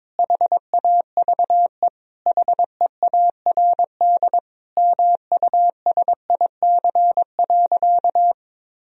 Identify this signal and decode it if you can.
Morse code, heard as a keyed tone, about 22 wpm, 700 Hz. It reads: HAVE HEARD MUSIC.
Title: Sentences from Top 1000 Words - Review Part 1 - 22wpm